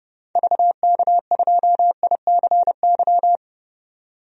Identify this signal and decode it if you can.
Morse code, heard as a keyed tone, about 30 wpm, 700 Hz. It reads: VK2ICY
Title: DXCC-100 Call Signs - 6 Characters - 30wpm